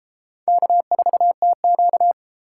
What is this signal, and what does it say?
Morse code, heard as a keyed tone, about 32 wpm, 700 Hz. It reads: K4TQ